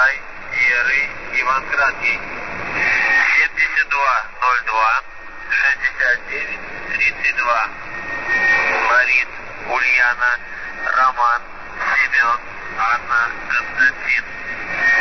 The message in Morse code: none

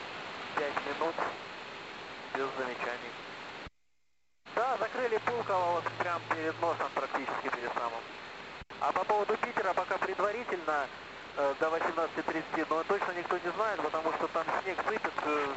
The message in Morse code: none

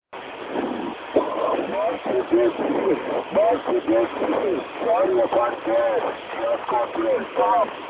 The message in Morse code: none

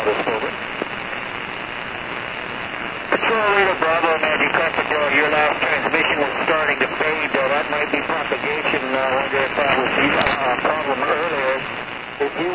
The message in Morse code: none

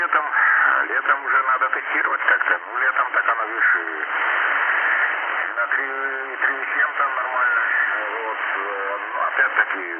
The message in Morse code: none

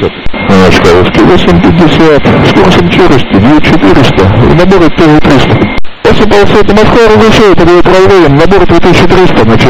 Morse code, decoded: none

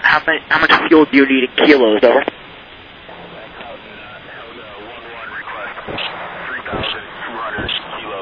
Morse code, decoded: none

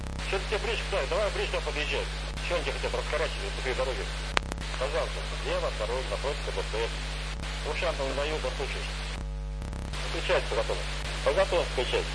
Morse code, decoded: none

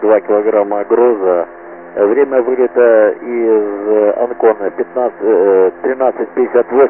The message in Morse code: none

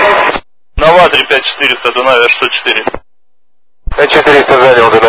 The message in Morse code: none

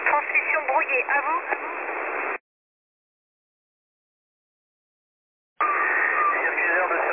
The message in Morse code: none

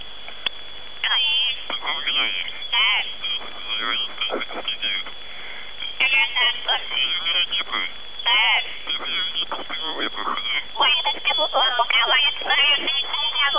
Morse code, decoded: none

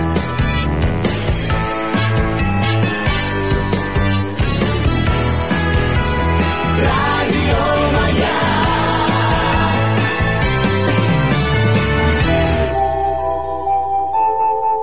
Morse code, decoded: none

